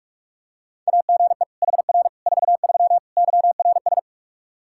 Morse code, heard as a keyed tone, about 46 wpm, 700 Hz. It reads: AGE HR 43 YRS
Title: Practice QSO - Exchange 3 - More Info - 45wpm